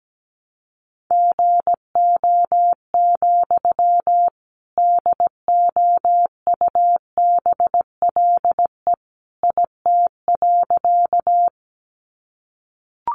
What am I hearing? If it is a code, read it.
Morse code, heard as a keyed tone, about 17 wpm, 700 Hz. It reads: GO, DOUBLE IT.
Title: Sentences from Top 1400 Words 17wpm